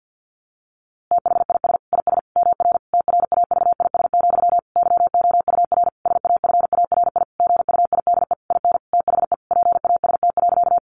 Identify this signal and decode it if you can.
Morse code, heard as a keyed tone, about 50 wpm, 700 Hz. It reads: THIS IS MR TRAVIS, YOUR SAFARI GUIDE IN THE PAST.
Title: A Sound Of Thunder 50wpm